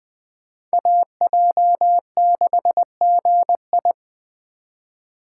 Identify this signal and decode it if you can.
Morse code, heard as a keyed tone, about 20 wpm, 700 Hz. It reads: AJ6GI
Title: USA Call Signs - 5 Characters - 2x Character Spacing - 20wpm